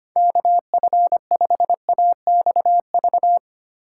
Morse code, heard as a keyed tone, about 25 wpm, 690 Hz. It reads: KF5AXV
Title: USA Call Signs - 6 Characters - 2x Character Spacing - 25wpm